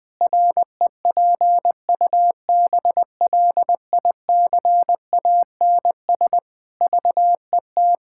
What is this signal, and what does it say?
Morse code, heard as a keyed tone, about 21 wpm, 700 Hz. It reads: REPUBLICANS VET
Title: Pangram Challenge v2 - 20wpm